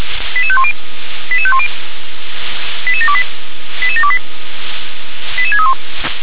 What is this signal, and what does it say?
Five-tone selective call (ZVEI), heard as quick runs of five tones, tones of about 70 ms. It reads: 88510 88510 88728 88728 88631